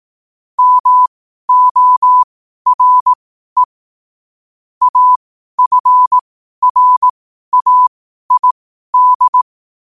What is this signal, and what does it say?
Morse code, heard as a keyed tone, about 18 wpm, spaced as for 12 wpm, 1000 Hz. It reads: MORE AFRAID
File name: qotdfiles / 12wpm